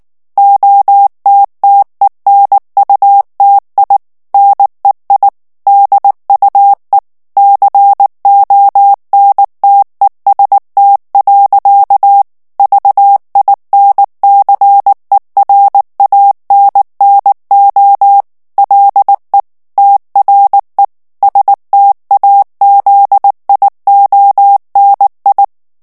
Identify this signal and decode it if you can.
Morse code, heard as a keyed tone, about 19 wpm, 800 Hz. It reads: OTTENUTI NEI DUE CONTEST. VINCERANNO LE TRE STAZIONI
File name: morse mp3 / ITALIEN texte clair 2 vitesse 18 mots minute